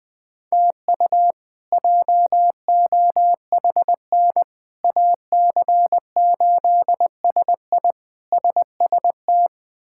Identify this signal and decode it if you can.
Morse code, heard as a keyed tone, about 20 wpm, 700 Hz. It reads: TU JOHN AC8SI SST
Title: K1USN SST QSO Practice - 20wpm